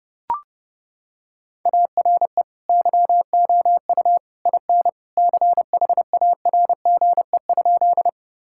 Morse code, heard as a keyed tone, about 32 wpm, 700 Hz. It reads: ARE YOU IN CHARGE?
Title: Sentences from Top 1000 Words - Review Part 6 - 30wpm